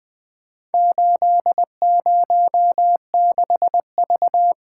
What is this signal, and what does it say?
Morse code, heard as a keyed tone, about 20 wpm, 700 Hz. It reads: 806V
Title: Four Letter-Number Combinations - 20wpm